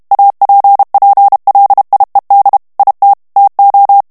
Morse code, heard as a keyed tone, about 32 wpm, 800 Hz. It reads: APPLIED IT TO
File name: morse mp3 / ANGLAIS texte clair 3 vitesse 30 mots minute